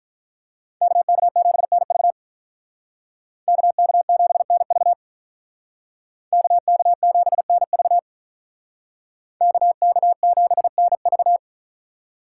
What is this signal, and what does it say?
Morse code, heard as a keyed tone, about 43 wpm, 700 Hz. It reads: KK7NV KK7NV KK7NV KK7NV
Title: Speed Racing - USA Call Signs 35wpm